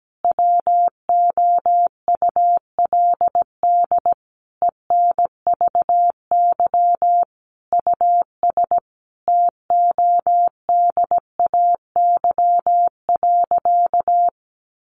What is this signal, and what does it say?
Morse code, heard as a keyed tone, about 17 wpm, 700 Hz. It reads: WOULD ENVY US TODAY.
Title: A Sound Of Thunder 17wpm